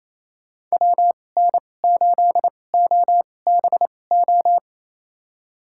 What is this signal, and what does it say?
Morse code, heard as a keyed tone, about 28 wpm, spaced as for 20 wpm, 700 Hz. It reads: WN8OBO
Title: USA Call Signs - 6 Characters - 2x Character Spacing - 28wpm